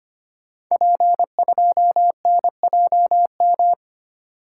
Morse code, heard as a keyed tone, about 25 wpm, 700 Hz. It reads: P2NJM
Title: Code Groups - 5 Letter:Numbers - 25wpm